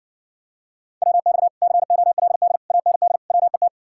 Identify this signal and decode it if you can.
Morse code, heard as a keyed tone, about 70 wpm, 700 Hz. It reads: WX COLD AND WET